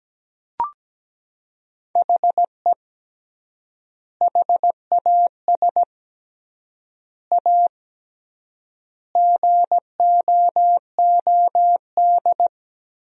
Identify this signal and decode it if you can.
Morse code, heard as a keyed tone, about 17 wpm, 700 Hz. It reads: HE HAS A GOOD
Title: 3x Word-Spacing - Sentences from Top 500 Words - 17wpm